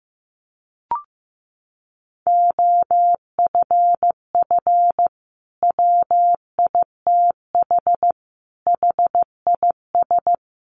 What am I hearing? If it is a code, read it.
Morse code, heard as a keyed tone, about 15 wpm, 700 Hz. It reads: OFF WITH HIS
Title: Sentences from Top 300 Words 15wpm